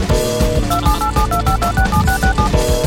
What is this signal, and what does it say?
DTMF, heard as telephone touch tones, tones of about 110 ms. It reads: 278725267667